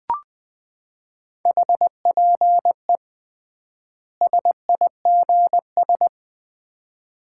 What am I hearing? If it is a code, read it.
Morse code, heard as a keyed tone, about 21 wpm, 700 Hz. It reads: HPE SIGS
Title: Common QSO Phrases - 3x Word-Spacing - 20wpm